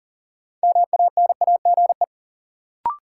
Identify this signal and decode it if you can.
Morse code, heard as a keyed tone, about 40 wpm, 700 Hz. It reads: MANAGE
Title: Sentences from Top 1400 Words - New Word Drill - 40wpm